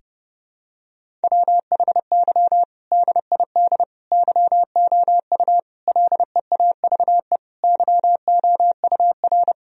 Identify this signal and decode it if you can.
Morse code, heard as a keyed tone, about 32 wpm, 700 Hz. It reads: WHY DID YOU LEAVE YOUR
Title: Sentences from Top 1000 Words - Review Part 3 - 30wpm